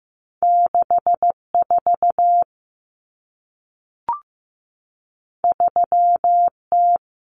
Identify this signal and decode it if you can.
Morse code, heard as a keyed tone, about 15 wpm, 700 Hz. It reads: 64 3T
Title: Common Cut Numbers - 2-Digit 15wpm